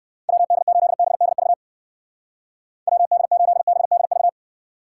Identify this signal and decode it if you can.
Morse code, heard as a keyed tone, about 57 wpm, 700 Hz. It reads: WD8BDV WD8BDV